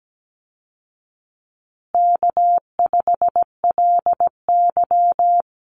Morse code, heard as a keyed tone, about 17 wpm, 700 Hz. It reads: K5LY
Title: Easy Call Signs Repeated 3x - 17wpm